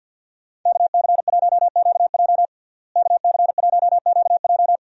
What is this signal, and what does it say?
Morse code, heard as a keyed tone, about 49 wpm, 700 Hz. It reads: KC1QJ KC1QJ
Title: USA Call Signs - 5 Characters - Repeated 2x - 50wpm